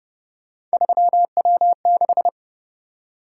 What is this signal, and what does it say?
Morse code, heard as a keyed tone, about 29 wpm, 700 Hz. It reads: 3W6